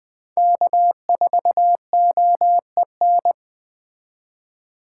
Morse code, heard as a keyed tone, about 20 wpm, 700 Hz. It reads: K4OEN
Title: Speed Racing - USA 1x3 Call Signs 17wpm